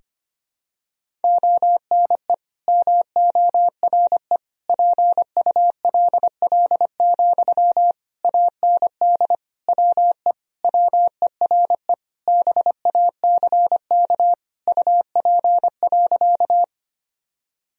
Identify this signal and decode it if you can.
Morse code, heard as a keyed tone, about 25 wpm, 700 Hz. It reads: ONE MORE PULL, AND WE WERE BACK UP.
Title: Sentences from Top 1000 Words - Review Part 2 - 25wpm